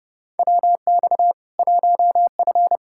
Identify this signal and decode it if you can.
Morse code, heard as a keyed tone, about 30 wpm, 700 Hz. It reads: WX 1F
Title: Practice QSO - Exchange 2 - Info - 30wpm